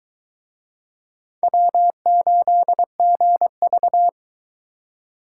Morse code, heard as a keyed tone, about 20 wpm, 700 Hz. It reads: W8GV